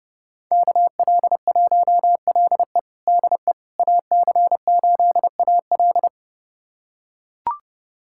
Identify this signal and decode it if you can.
Morse code, heard as a keyed tone, about 30 wpm, 700 Hz. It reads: KL1LE DE AC8AL
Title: USA Callsign DE Callsign - 5 Characters - 30wpm